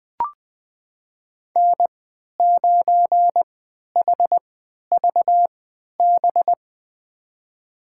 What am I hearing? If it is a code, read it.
Morse code, heard as a keyed tone, about 21 wpm, spaced as for 11 wpm, 700 Hz. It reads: N9HVB
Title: USA Call Signs - 3x Character Spacing - 20wpm